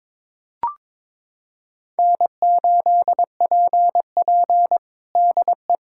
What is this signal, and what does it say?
Morse code, heard as a keyed tone, about 22 wpm, 700 Hz. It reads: N8PP DE
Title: USA Callsign DE Callsign - 4 Characters - 22wpm